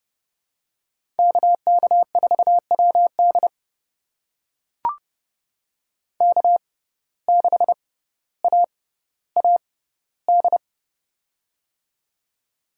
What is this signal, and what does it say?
Morse code, heard as a keyed tone, about 30 wpm, spaced as for 15 wpm, 700 Hz. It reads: KK4WD K6AAD
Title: USA Call Signs - 5 Characters - 6x Character Spacing - 30wpm